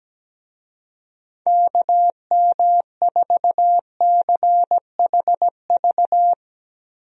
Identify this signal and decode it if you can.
Morse code, heard as a keyed tone, about 17 wpm, 700 Hz. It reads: KM4CHV
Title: USA Call Signs - 6 Characters - Repeated 2x - 17wpm